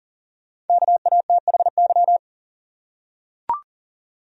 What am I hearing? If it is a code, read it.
Morse code, heard as a keyed tone, about 40 wpm, 700 Hz. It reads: KATHY
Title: USA Common Names - 40wpm